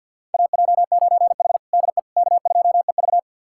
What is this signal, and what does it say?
Morse code, heard as a keyed tone, about 50 wpm, 690 Hz. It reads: AJ9H DE K2EV
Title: USA Callsign DE Callsign - 4 Characters - 50wpm